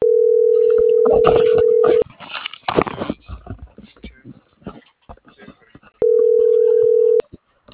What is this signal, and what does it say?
Ringback tone (North America), 440 + 480 Hz.